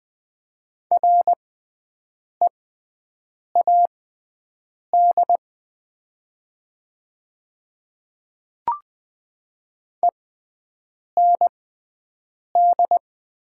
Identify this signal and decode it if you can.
Morse code, heard as a keyed tone, about 20 wpm, spaced as for 7 wpm, 700 Hz. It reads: READ END
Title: 6x Character Spacing - Top 200 Words - 20wpm